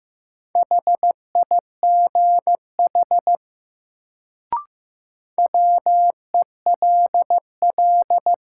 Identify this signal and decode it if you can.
Morse code, heard as a keyed tone, about 15 wpm, 700 Hz. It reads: HIGH WELL
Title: Sentences from Top 200 Words - New Word Drill - 15wpm